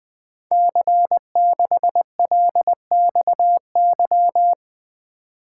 Morse code, heard as a keyed tone, about 21 wpm, 700 Hz. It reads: C6LXY
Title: Code Groups - 5 Letter:Numbers - 20wpm